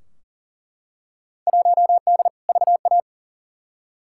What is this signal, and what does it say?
Morse code, heard as a keyed tone, about 40 wpm, 700 Hz. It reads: 1D VA
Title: Field Day Exchanges - 40wpm